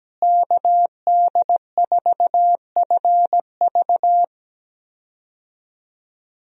Morse code, heard as a keyed tone, about 17 wpm, 700 Hz. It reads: KD4FV
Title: USA Call Signs - 5 Characters - Repeated 2x - 17wpm